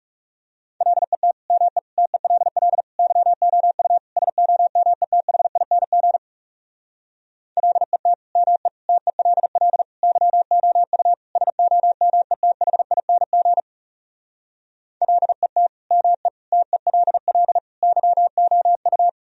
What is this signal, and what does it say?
Morse code, heard as a keyed tone, about 40 wpm, 700 Hz. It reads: LET ME TELL YOU SOMETHING LET ME TELL YOU SOMETHING LET ME TELL YOU